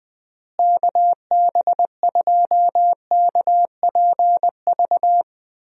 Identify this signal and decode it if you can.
Morse code, heard as a keyed tone, about 19 wpm, 700 Hz. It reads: KB2KPV
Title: USA Call Signs - 6 Characters - Repeated 3x - 20wpm